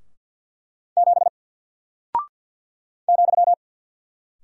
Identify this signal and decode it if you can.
Morse code, heard as a keyed tone, about 54 wpm, 700 Hz. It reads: 7 ,
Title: Single Character (Letter:Number:Punctuation:Prosign) Alt 50wpm